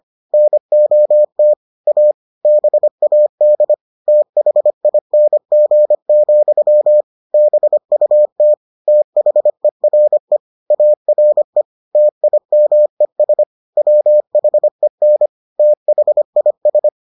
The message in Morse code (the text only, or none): NOT A BAD THING, BUT THERE ARE TIMES WHEN THIS